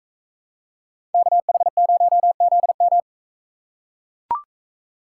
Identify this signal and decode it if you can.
Morse code, heard as a keyed tone, about 42 wpm, 700 Hz. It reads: KH0ZM